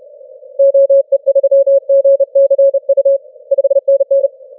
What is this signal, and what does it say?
Morse code, heard as a keyed tone, about 29 wpm, 550 Hz. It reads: OE3GCU 5NN
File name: wp / 9m4sll 15m cw 10 feb 2013 1452z